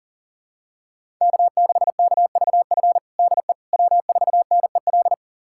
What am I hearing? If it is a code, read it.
Morse code, heard as a keyed tone, about 37 wpm, 700 Hz. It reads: K6KVF DE W4NEL
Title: USA Callsign DE Callsign - 5 Characters - 40wpm